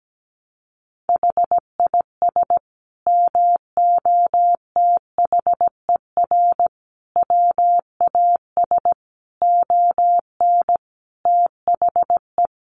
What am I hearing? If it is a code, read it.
Morse code, heard as a keyed tone, about 17 wpm, 700 Hz. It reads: HIS MOTHER WAS ON THE
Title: Sentences from Top 1200 Words 17wpm